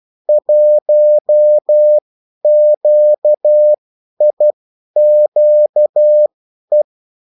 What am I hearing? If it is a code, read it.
Morse code, heard as a keyed tone, about 12 wpm, spaced as for 10 wpm, 600 Hz. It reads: 1QIQE
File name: morse mp3 / revision9-vitesse 10 mots